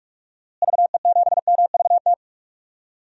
Morse code, heard as a keyed tone, about 47 wpm, 700 Hz. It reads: VE7MVT
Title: DXCC-100 Call Signs - 6 Characters - Repeated 2x - 45wpm